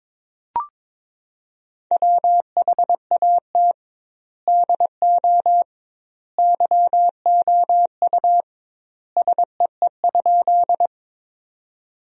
Morse code, heard as a keyed tone, about 22 wpm, 700 Hz. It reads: WHAT DO YOU SEE?
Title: 2x Word-Spacing - Sentences from Top 100 Words - 22wpm